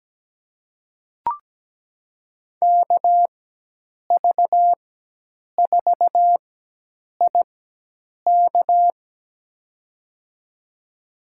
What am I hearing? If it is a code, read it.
Morse code, heard as a keyed tone, about 17 wpm, spaced as for 8 wpm, 700 Hz. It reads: KV4IK
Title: USA Call Signs - 5 Characters - 4x Character Spacing - 17wpm